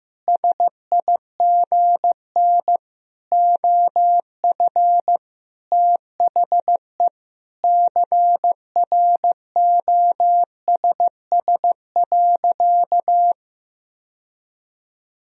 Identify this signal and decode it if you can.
Morse code, heard as a keyed tone, about 15 wpm, 700 Hz. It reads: SIGN OF THE CROSS.
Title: Sentences from Top 1000 Words - Review Part 1 - 15wpm